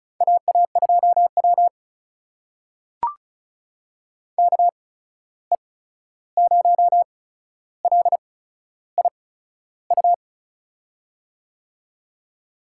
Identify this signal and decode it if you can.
Morse code, heard as a keyed tone, about 36 wpm, spaced as for 13 wpm, 700 Hz. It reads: AA2W KE0LIU
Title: USA Call Signs - 8x Character Spacing - 35wpm